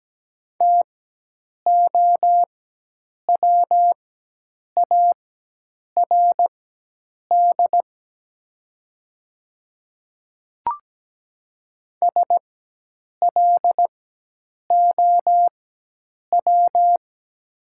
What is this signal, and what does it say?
Morse code, heard as a keyed tone, about 17 wpm, spaced as for 8 wpm, 700 Hz. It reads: TOWARD SLOW
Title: 4x Character Spacing - Top 500 Words - 17wpm